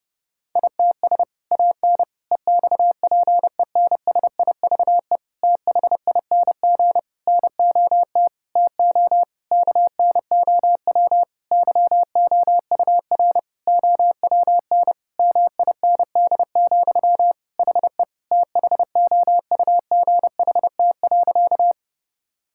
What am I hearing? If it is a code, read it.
Morse code, heard as a keyed tone, about 30 wpm, 700 Hz. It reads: ITS AN EXPENSIVE THING NOT TO KNOW YOUR OWN MIND, HE THOUGHT.
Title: Sentences from Top 1000 Words - Review Part 6 - 30wpm